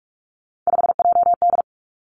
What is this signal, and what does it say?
Morse code, heard as a keyed tone, about 46 wpm, 700 Hz. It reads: 5JD
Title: Three Challenging Letter-Number Combinations 45wpm